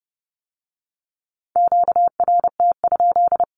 Morse code, heard as a keyed tone, about 30 wpm, 700 Hz. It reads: QRT?
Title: Q-Codes with Spoken Meaning - 30wpm